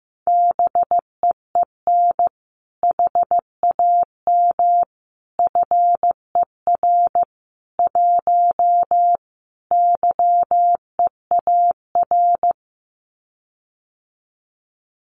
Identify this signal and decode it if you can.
Morse code, heard as a keyed tone, about 15 wpm, 700 Hz. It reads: BEEN HAM FER 1 YEAR
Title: Common QSO Phrases Encore 15wpm